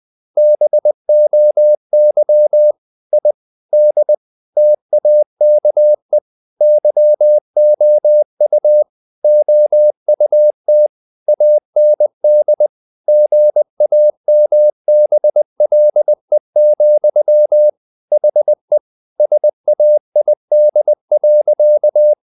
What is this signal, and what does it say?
Morse code, heard as a keyed tone, about 20 wpm, 600 Hz. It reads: BOY I D TAKE YOU OUT AND GAMBLE, HE SAID.
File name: the old man and the sea / the old man and the sea part 1 20wpm